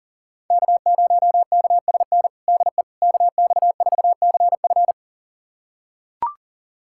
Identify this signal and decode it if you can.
Morse code, heard as a keyed tone, about 40 wpm, 700 Hz. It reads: K0KSN DE KX4CF
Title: USA Callsign DE Callsign 40wpm